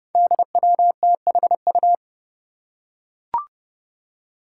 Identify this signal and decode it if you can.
Morse code, heard as a keyed tone, about 32 wpm, 700 Hz. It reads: DWTHU
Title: Five Challenging Letter-Number Combinations 30wpm